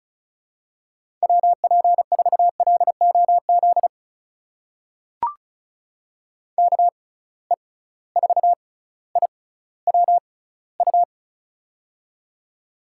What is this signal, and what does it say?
Morse code, heard as a keyed tone, about 36 wpm, spaced as for 18 wpm, 700 Hz. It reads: WP4LOZ KE4IWU
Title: USA Call Signs - 6 Characters - 6x Character Spacing - 35wpm